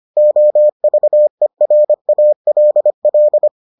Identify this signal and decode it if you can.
Morse code, heard as a keyed tone, about 25 wpm, 600 Hz. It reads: OVERALL